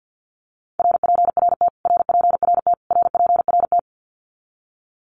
Morse code, heard as a keyed tone, about 50 wpm, 700 Hz. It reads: RPRT RPRT RPRT
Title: Abbreviations Repeated 3x - 50wpm